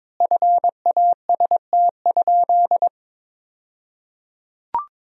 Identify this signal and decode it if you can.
Morse code, heard as a keyed tone, about 22 wpm, 700 Hz. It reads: FAST?
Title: 2x Word-Spacing - Sentences from Top 400 Words - 22wpm